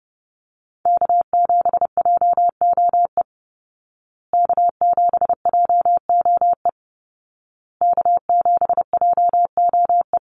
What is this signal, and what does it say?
Morse code, heard as a keyed tone, about 30 wpm, 700 Hz. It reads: K7JOE K7JOE K7JOE
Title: Call Signs Repeated 2x - 30wpm